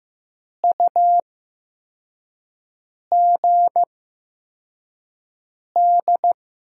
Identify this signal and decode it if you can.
Morse code, heard as a keyed tone, about 15 wpm, spaced as for 4 wpm, 700 Hz. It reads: UGD
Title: USA Call Signs - 6 Characters - 8x Character Spacing - 15wpm